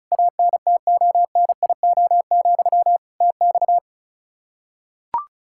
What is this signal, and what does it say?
Morse code, heard as a keyed tone, about 36 wpm, 700 Hz. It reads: ANTONIO, TX
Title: K1USN SST QSO Practice 35wpm